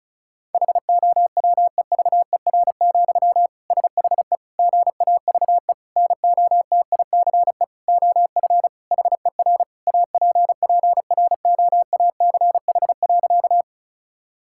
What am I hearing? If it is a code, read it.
Morse code, heard as a keyed tone, about 36 wpm, 700 Hz. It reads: HOWEVER, SHE GAVE NOTICE OF HER APPROACH.
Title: Sentences from Top 1000 Words - Review Part 6 - 35wpm